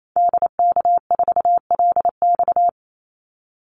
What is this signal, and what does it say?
Morse code, heard as a keyed tone, about 28 wpm, 700 Hz. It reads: DK4LX
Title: Call Signs - 28wpm